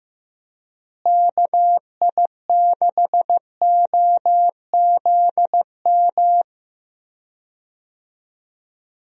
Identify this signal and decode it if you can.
Morse code, heard as a keyed tone, about 15 wpm, 700 Hz. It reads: KI6OZM
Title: USA Call Signs - 6 Characters - Repeated 3x - 15wpm